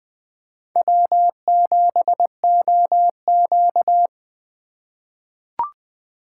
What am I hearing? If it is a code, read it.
Morse code, heard as a keyed tone, about 21 wpm, 700 Hz. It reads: W7OQ
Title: K1USN SST QSO Practice - 20wpm